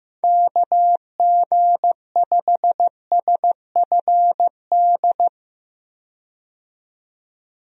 Morse code, heard as a keyed tone, about 15 wpm, 700 Hz. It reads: KG5SFD
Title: USA Call Signs - 6 Characters - 15wpm